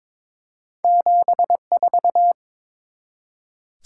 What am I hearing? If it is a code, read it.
Morse code, heard as a keyed tone, about 23 wpm, 700 Hz. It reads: 74
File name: practice / 22wpm / Two Letter-Number Combinations - 22wpm